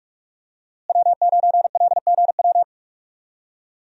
Morse code, heard as a keyed tone, about 44 wpm, 700 Hz. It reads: W9LGW
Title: USA Call Signs - 5 Characters - Repeated 2x - 45wpm